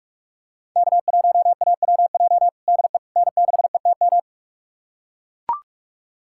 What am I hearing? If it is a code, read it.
Morse code, heard as a keyed tone, about 43 wpm, 700 Hz. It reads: K1AWJ DE N6ETM
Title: USA Callsign DE Callsign 45wpm